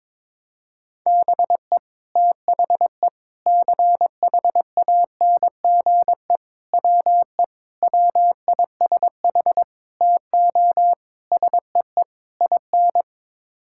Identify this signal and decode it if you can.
Morse code, heard as a keyed tone, about 22 wpm, 700 Hz. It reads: BE THE CHANGE WE WISH TO SEE IN
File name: practice / 22wpm / Sets of 8 Words - 22wpm